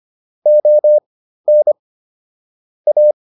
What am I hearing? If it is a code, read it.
Morse code, heard as a keyed tone, about 25 wpm, spaced as for 13 wpm, 600 Hz. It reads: ON A